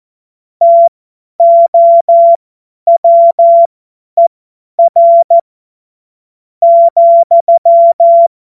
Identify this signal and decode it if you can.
Morse code, heard as a keyed tone, about 14 wpm, spaced as for 9 wpm, 680 Hz. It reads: TOWER,